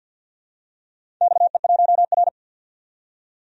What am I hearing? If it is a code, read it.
Morse code, heard as a keyed tone, about 48 wpm, 700 Hz. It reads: XE1R
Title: DXCC-100 Call Signs - 4 Characters - 50wpm